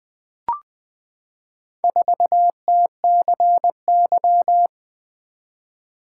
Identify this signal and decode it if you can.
Morse code, heard as a keyed tone, about 20 wpm, 700 Hz. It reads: 4TCY
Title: Four Letter-Number Combinations - 20wpm